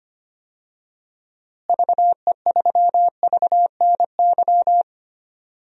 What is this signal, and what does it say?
Morse code, heard as a keyed tone, about 25 wpm, 700 Hz. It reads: VE3VNY
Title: DXCC-100 Call Signs - 6 Characters - Repeated 3x - 25wpm